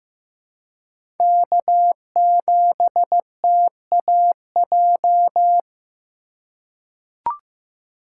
Morse code, heard as a keyed tone, about 15 wpm, 700 Hz. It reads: K7TAJ
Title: USA Call Signs - 5 Characters - 4x Character Spacing - 15wpm